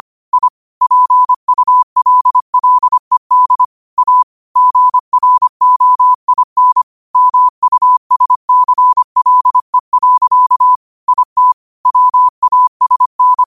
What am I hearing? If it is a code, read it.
Morse code, heard as a keyed tone, about 25 wpm, 1000 Hz. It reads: I PULLED A GROIN MUSCLE. IT WASN
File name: qotdfiles / 25wpm